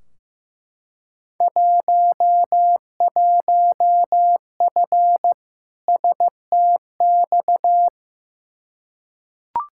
Morse code, heard as a keyed tone, about 16 wpm, 700 Hz. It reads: 11F STX
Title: Field Day Exchanges Encore - 15wpm